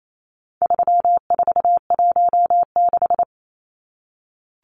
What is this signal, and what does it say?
Morse code, heard as a keyed tone, about 28 wpm, 700 Hz. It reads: 3416
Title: Numbers - 4-Digit 28wpm